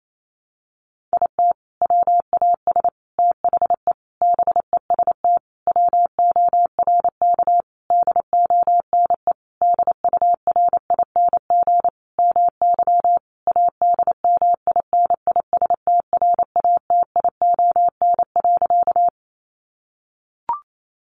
Morse code, heard as a keyed tone, about 28 wpm, 700 Hz. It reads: IT WAS THE BEST WORK DONE DURING MY ADMINISTRATION.
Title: Sentences from Top 1300 Words 28wpm